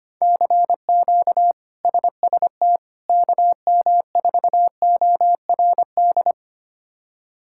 Morse code, heard as a keyed tone, about 25 wpm, 700 Hz. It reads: CQ SST KM4ORD
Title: K1USN SST QSO Practice - 25wpm